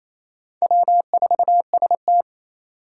Morse code, heard as a keyed tone, about 28 wpm, 700 Hz. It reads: W4ST
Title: Four Challenging Letter-Number Combinations 28wpm